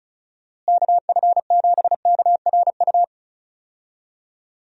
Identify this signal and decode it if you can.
Morse code, heard as a keyed tone, about 36 wpm, 700 Hz. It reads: KF7KRU
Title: USA Call Signs - 6 Characters - Repeated 3x - 35wpm